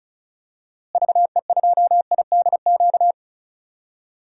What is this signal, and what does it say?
Morse code, heard as a keyed tone, about 34 wpm, 700 Hz. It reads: VE2IDQ
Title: DXCC-100 Call Signs - 6 Characters - Repeated 2x - 35wpm